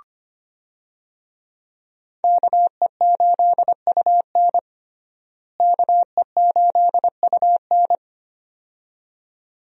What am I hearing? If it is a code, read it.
Morse code, heard as a keyed tone, about 25 wpm, 700 Hz. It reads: KE8UN KE8UN
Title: USA Call Signs - 5 Characters - Repeated 2x - 25wpm